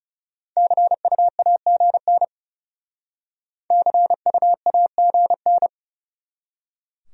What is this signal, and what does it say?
Morse code, heard as a keyed tone, about 39 wpm, 700 Hz. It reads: CUAGN CUAGN